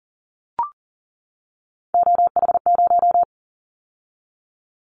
Morse code, heard as a keyed tone, about 40 wpm, 700 Hz. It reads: OH0
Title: Callsign Prefixes for Top 100 DXCC - 40wpm